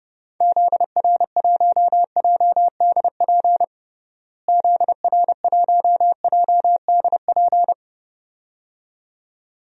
Morse code, heard as a keyed tone, about 32 wpm, 700 Hz. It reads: ZR1JDP ZR1JDP
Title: DXCC-100 Call Signs - 6 Characters - Repeated 2x - 30wpm